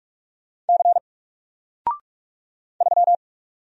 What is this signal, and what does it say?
Morse code, heard as a keyed tone, about 46 wpm, 700 Hz. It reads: C 3
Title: Single Letter-Number 45wpm